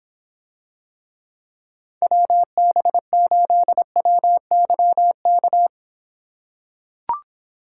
Morse code, heard as a keyed tone, about 26 wpm, 700 Hz. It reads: WB8WYK